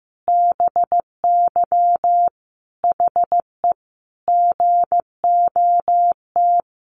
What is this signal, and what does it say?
Morse code, heard as a keyed tone, about 15 wpm, 700 Hz. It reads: BY HE GOT
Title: Sentences from Top 300 Words 15wpm